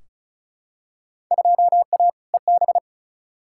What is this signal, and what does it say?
Morse code, heard as a keyed tone, about 36 wpm, 700 Hz. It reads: 2A EB
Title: Field Day Exchanges - 35wpm